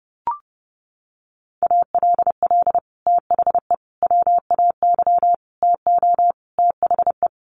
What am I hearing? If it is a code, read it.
Morse code, heard as a keyed tone, about 30 wpm, 700 Hz. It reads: ALL THE WAY TO THE
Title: Sets of 5 Words 30wpm